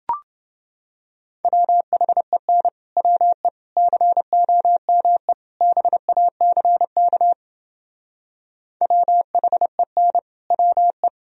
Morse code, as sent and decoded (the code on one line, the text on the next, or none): .-- .... -. / .-- . / -.-. --- -- . / -... .- -.-. -.- / .-- .... -. / .-- .
WHEN WE COME BACK WHEN WE